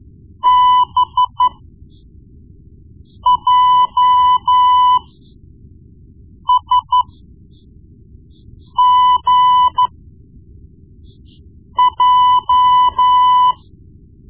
Morse code, heard as a keyed tone, about 12 wpm, spaced as for 4 wpm, 1000 Hz. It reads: BJSGJ
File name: Morse Lessons / Morse Lessons / Tape-1B